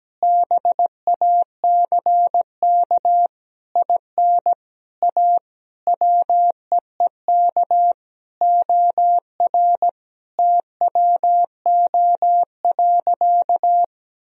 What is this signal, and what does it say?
Morse code, heard as a keyed tone, about 17 wpm, 700 Hz. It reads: BACK IN A WEEK OR TWO.
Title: Sentences from Top 1000 Words - Review Part 2 - 17wpm